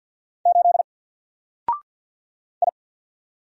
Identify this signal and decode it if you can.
Morse code, heard as a keyed tone, about 48 wpm, 700 Hz. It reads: 8 I
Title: Single Letter-Number Alt 50wpm